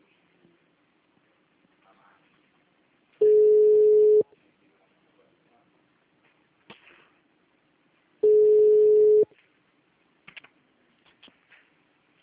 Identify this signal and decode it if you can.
Ringback tone (Europe, 425 Hz), with speech.